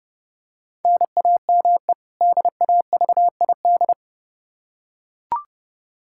Morse code, heard as a keyed tone, about 30 wpm, 700 Hz. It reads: NAME DAVID
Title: K1USN SST QSO Practice 30wpm